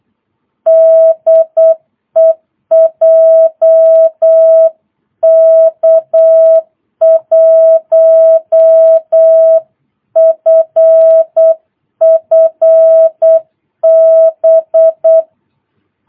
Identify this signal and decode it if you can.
Morse code, heard as a keyed tone, about 8 wpm, 650 Hz. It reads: DE JK1FFB
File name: jh8cbh / 2022cw